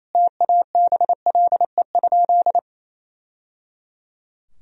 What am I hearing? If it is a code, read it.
Morse code, heard as a keyed tone, about 28 wpm, 700 Hz. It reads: TABLE?
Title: Sentences from Top 1000 Words - Review Part 2 - 28wpm